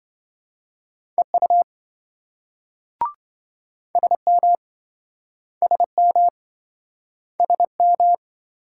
Morse code, heard as a keyed tone, about 28 wpm, 700 Hz. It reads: EU SM SM SM